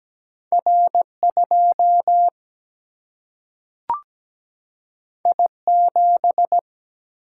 Morse code, heard as a keyed tone, about 17 wpm, 700 Hz. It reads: R 2 I 7
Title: Single-Letter Single-Number 17wpm